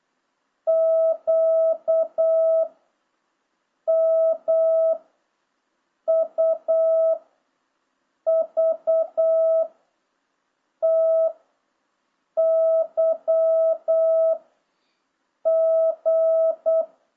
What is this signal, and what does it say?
Morse code, heard as a keyed tone, about 8 wpm, 660 Hz. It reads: Q M U V T Y G